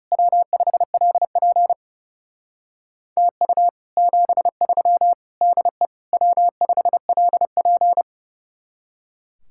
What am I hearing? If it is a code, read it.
Morse code, heard as a keyed tone, about 39 wpm, 700 Hz. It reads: W5LP TU 73 DE W5LP